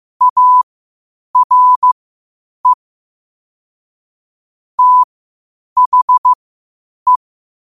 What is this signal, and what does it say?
Morse code, heard as a keyed tone, about 15 wpm, spaced as for 8 wpm, 1000 Hz. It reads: ARE THE
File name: qotdfiles / 8wpm